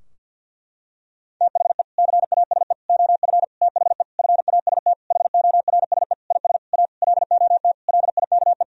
Morse code, heard as a keyed tone, about 46 wpm, 700 Hz. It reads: THE CASE OF THE FAST HORSE IS A LOT LIKE
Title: Sentences from Top 1000 Words - Review Part 2 - 50wpm